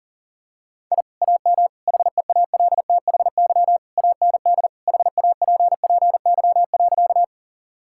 Morse code, heard as a keyed tone, about 40 wpm, 700 Hz. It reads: I AM HEALTHY AND HAPPY.
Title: Sentences from Top 1000 Words - Review Part 5 - 40wpm